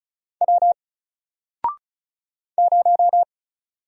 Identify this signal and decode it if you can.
Morse code, heard as a keyed tone, about 36 wpm, 700 Hz. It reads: W 0
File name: Single-Letter-Number-35wpm